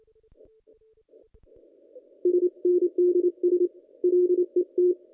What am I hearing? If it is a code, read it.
Morse code, heard as a keyed tone, about 27 wpm, 360 Hz. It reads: SNDS LET
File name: contests / ss-cw-k5tr-op-n5rz -1800